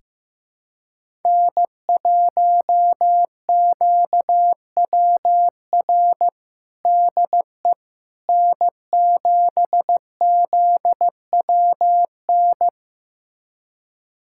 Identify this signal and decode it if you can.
Morse code, heard as a keyed tone, about 15 wpm, 700 Hz. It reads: N1QWR DE N7ZWN
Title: USA Callsign DE Callsign - 5 Characters - 15wpm